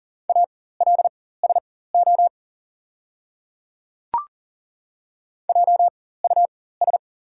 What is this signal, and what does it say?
Morse code, heard as a keyed tone, about 40 wpm, spaced as for 19 wpm, 690 Hz. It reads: ALSO JUS
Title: 4x Character Spacing - Top 200 Words - 40wpm